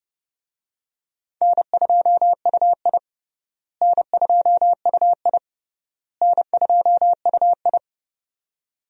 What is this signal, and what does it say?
Morse code, heard as a keyed tone, about 30 wpm, 700 Hz. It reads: N2UI N2UI N2UI
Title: USA Call Signs - 4 Characters - Repeated 3x - 30wpm